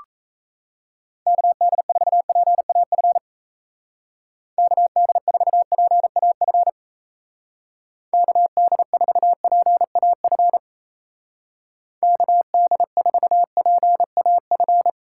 Morse code, heard as a keyed tone, about 34 wpm, 700 Hz. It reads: KD4PAF KD4PAF KD4PAF KD4PAF
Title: Speed Racing - USA 2x3 Call Signs 28wpm